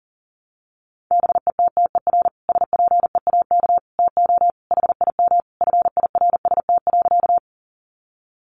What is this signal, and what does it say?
Morse code, heard as a keyed tone, about 37 wpm, 700 Hz. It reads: BETTER SPEAK TO HIM FIRST.
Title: Sentences from Top 1100 Words 40wpm